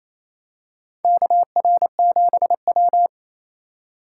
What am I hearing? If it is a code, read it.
Morse code, heard as a keyed tone, about 28 wpm, 700 Hz. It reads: KR7W
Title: K1USN SST QSO Practice - 28wpm